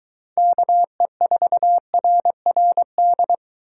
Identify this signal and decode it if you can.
Morse code, heard as a keyed tone, about 23 wpm, 700 Hz. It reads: KE4RRD